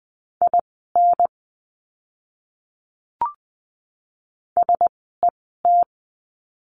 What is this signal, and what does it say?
Morse code, heard as a keyed tone, about 20 wpm, spaced as for 14 wpm, 700 Hz. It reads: IN SET
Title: 036 Lesson 7f - Letter s - Words 20wpm